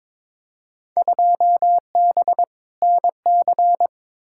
Morse code, heard as a keyed tone, about 21 wpm, 700 Hz. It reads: 2B NC
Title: Field Day Exchanges Encore - 22wpm